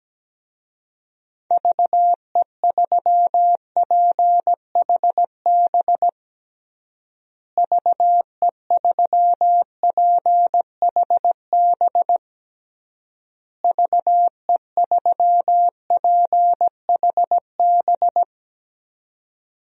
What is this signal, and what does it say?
Morse code, heard as a keyed tone, about 17 wpm, 700 Hz. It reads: VE3PHB VE3PHB VE3PHB
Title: DXCC-100 Call Signs - 6 Characters - Repeated 3x - 17wpm